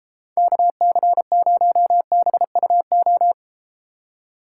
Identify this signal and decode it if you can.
Morse code, heard as a keyed tone, about 32 wpm, 700 Hz. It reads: KC0BUO